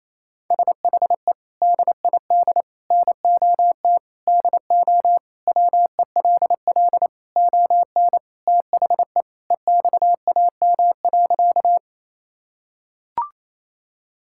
Morse code, heard as a keyed tone, about 28 wpm, 700 Hz. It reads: SHE DID NOT DO WELL ON THE EXAM.
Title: Sentences from Top 1000 Words - Review Part 5 - 28wpm